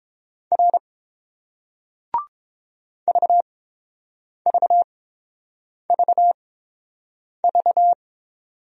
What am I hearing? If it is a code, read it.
Morse code, heard as a keyed tone, about 29 wpm, spaced as for 7 wpm, 690 Hz. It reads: R VVVV